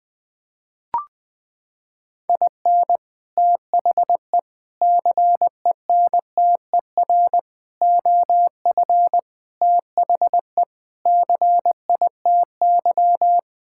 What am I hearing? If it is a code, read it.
Morse code, heard as a keyed tone, about 21 wpm, 700 Hz. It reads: IN THE CENTER OF THE CITY
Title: Sentences with 5-Word Phrases - 20wpm